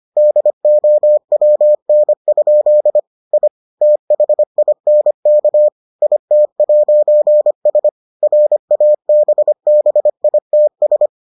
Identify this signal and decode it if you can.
Morse code, heard as a keyed tone, about 25 wpm, 600 Hz. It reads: DOWN? I THINK IT'S RABBITS